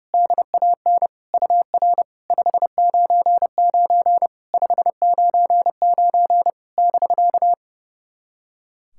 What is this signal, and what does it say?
Morse code, heard as a keyed tone, about 30 wpm, 700 Hz. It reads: DAN UR 599 599 <BK>
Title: SOTA QSO Practice 30wpm